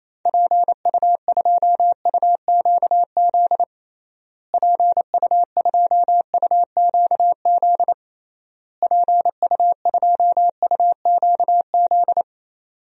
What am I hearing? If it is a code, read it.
Morse code, heard as a keyed tone, about 27 wpm, 700 Hz. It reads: PU2UQZ PU2UQZ PU2UQZ
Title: DXCC-100 Call Signs - 6 Characters - Repeated 3x - 28wpm